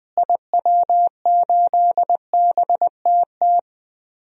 Morse code, heard as a keyed tone, about 20 wpm, 700 Hz. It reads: IW8BTT
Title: DXCC-100 Call Signs - 6 Characters - Repeated 3x - 20wpm